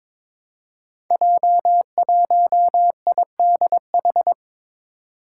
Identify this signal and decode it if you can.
Morse code, heard as a keyed tone, about 22 wpm, 700 Hz. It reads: J1IDH
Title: Five Challenging Letter-Number Combinations 22wpm